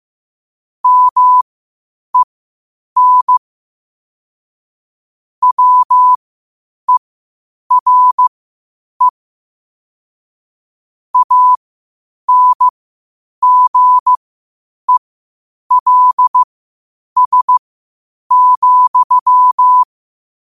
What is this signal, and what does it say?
Morse code, heard as a keyed tone, about 15 wpm, spaced as for 8 wpm, 1000 Hz. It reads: MEN WERE ANGELS,
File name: qotdfiles / 8wpm